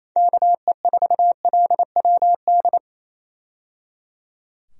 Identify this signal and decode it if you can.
Morse code, heard as a keyed tone, about 28 wpm, 700 Hz. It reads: KE4LWD
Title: USA Call Signs - 6 Characters - 28wpm